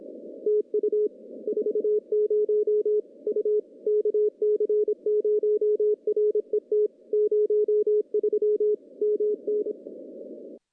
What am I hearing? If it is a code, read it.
Morse code, heard as a keyed tone, about 25 wpm, 420 Hz. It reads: TU 40 U KC0RET 03 MN
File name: contests / CW-SS-k5tr- -1400